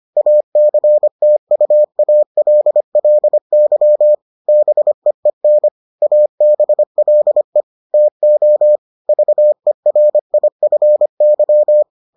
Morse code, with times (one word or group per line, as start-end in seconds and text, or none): 0.2-4.1 ACTUALLY
4.5-5.7 BEEN
6.0-7.6 ABLE
7.9-8.8 TO
9.1-11.8 VERIFY